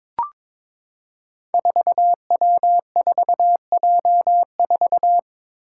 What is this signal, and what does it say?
Morse code, heard as a keyed tone, about 22 wpm, 700 Hz. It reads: 4W4J4
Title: Five Challenging Letter-Number Combinations 22wpm